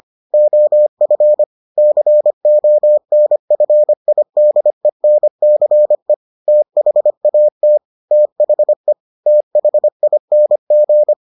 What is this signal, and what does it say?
Morse code, heard as a keyed tone, about 25 wpm, 600 Hz. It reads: OF CONFIDENCE THAT THE THING